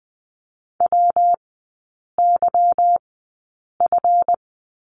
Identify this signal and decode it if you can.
Morse code, heard as a keyed tone, about 20 wpm, spaced as for 8 wpm, 700 Hz. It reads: WYF
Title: 152 Lesson 25e - Letter p - Three Characters 20wpm